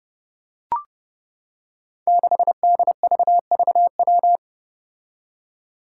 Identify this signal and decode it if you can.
Morse code, heard as a keyed tone, about 30 wpm, 700 Hz. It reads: 6DVVW
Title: Five Challenging Letter-Number Combinations 30wpm